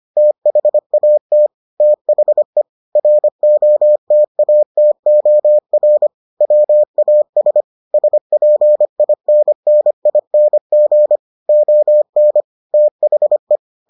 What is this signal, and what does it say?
Morse code, heard as a keyed tone, about 25 wpm, 600 Hz. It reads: THAT THE ROTATOR WAS SPINNING ON THE